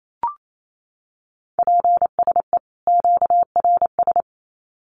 Morse code, heard as a keyed tone, about 28 wpm, 700 Hz. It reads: PSE QRS
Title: Common QSO Phrases 28wpm